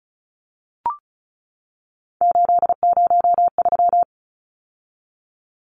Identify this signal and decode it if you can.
Morse code, heard as a keyed tone, about 34 wpm, 700 Hz. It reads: 803
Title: Numbers - 3-Digit 35wpm